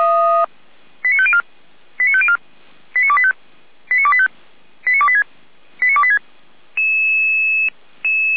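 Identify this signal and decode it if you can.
Five-tone selective call (ZVEI), heard as quick runs of five tones, tones of about 70 ms. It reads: 89504 89504 89285 89285 89286 89286
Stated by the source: ZVEI five-tone selective call received on SDR, from feuerwehr-orlamuende.org.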